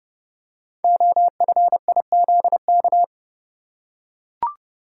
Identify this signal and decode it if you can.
Morse code, heard as a keyed tone, about 29 wpm, 700 Hz. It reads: OFIZK
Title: Code Groups - 5 Letters - 30wpm